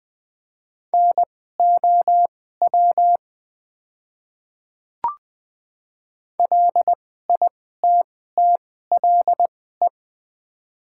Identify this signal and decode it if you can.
Morse code, heard as a keyed tone, about 20 wpm, spaced as for 14 wpm, 700 Hz. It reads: NOW LITTLE
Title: 2x Character Spacing - Top 300 Words - 20wpm